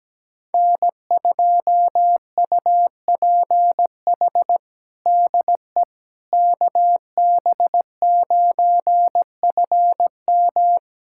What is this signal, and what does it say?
Morse code, heard as a keyed tone, about 17 wpm, 700 Hz. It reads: N2UPH DE KB9FM
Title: USA Callsign DE Callsign - 5 Characters - 17wpm